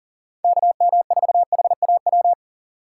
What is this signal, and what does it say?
Morse code, heard as a keyed tone, about 38 wpm, 700 Hz. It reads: KM4HAW
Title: Speed Racing - USA Call Signs 40wpm